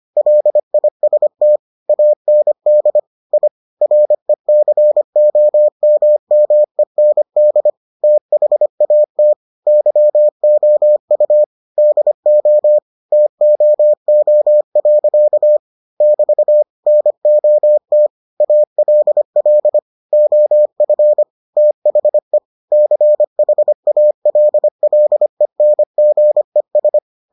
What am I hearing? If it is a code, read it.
Morse code, heard as a keyed tone, about 25 wpm, 600 Hz. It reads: LIST AND I RECOMMEND THAT YOU DO TOO. <BT> NOT ALL OF THE CHALLENGES